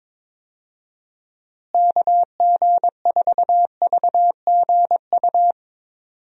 Morse code, heard as a keyed tone, about 22 wpm, 700 Hz. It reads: KG4VGU
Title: USA Call Signs - 6 Characters - Repeated 2x - 22wpm